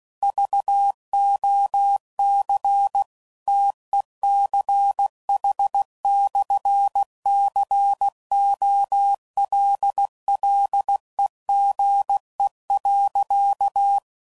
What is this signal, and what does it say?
Morse code, heard as a keyed tone, about 16 wpm, 800 Hz. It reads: VOC TECH/COLLEGE.
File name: CW Code Groups @15WPM / QSOs / WD0BGP Spaced